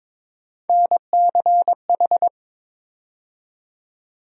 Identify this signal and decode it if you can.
Morse code, heard as a keyed tone, about 22 wpm, 700 Hz. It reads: NCH
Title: USA Call Sign Postfixes - 3 Letters 22wpm